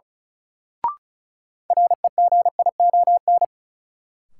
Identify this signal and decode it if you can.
Morse code, heard as a keyed tone, about 36 wpm, 700 Hz. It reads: REGION
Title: Sentences from Top 1500 Words - New Word Drill Alt - 35wpm